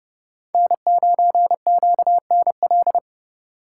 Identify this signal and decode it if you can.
Morse code, heard as a keyed tone, about 29 wpm, 700 Hz. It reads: N9QNL